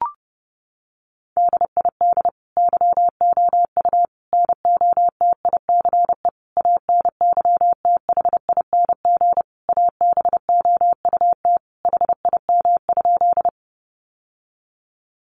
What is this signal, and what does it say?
Morse code, heard as a keyed tone, about 30 wpm, 700 Hz. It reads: DID YOU NOTICE ANYTHING ABOUT HIM?
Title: Sentences from Top 1000 Words 30wpm